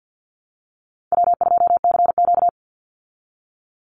Morse code, heard as a keyed tone, about 52 wpm, 700 Hz. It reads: W2CQ
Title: Easy Call Signs Repeated 2x - 50wpm